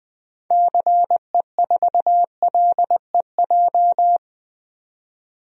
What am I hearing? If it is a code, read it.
Morse code, heard as a keyed tone, about 20 wpm, 700 Hz. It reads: CE4LEJ
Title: DXCC-100 Call Signs - 6 Characters - 20wpm